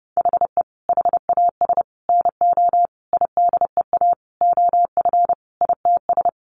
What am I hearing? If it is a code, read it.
Morse code, heard as a keyed tone, about 30 wpm, 700 Hz. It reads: HE HAS NO IDEA OF ITS
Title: Sentences from Top 1400 Words 30wpm